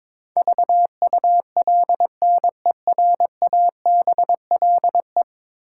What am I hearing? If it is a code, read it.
Morse code, heard as a keyed tone, about 23 wpm, 700 Hz. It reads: VULNERABLE
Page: morse-code-ninja-practice-22wpm